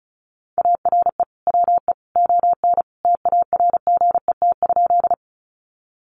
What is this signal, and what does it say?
Morse code, heard as a keyed tone, about 34 wpm, 700 Hz. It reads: ARE WE ON TARGET?
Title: Sentences from Top 1200 Words 35wpm